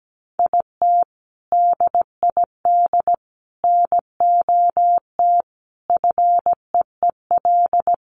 Morse code, heard as a keyed tone, about 17 wpm, 700 Hz. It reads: IT DID NOT FEEL
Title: Sentences from Top 1100 Words 17wpm